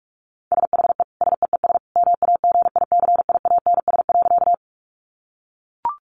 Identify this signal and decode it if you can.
Morse code, heard as a keyed tone, about 44 wpm, 700 Hz. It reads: SHE SEES MAGICIANS.
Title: Easy Phrases 45wpm